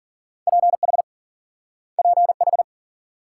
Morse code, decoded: PH PH

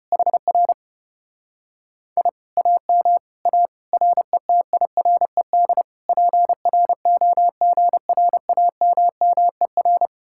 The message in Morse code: HR I AM A RETIRED PROGRAMMER

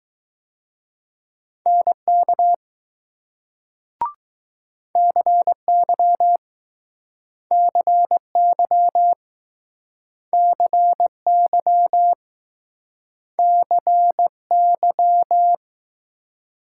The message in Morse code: NK CY CY CY CY

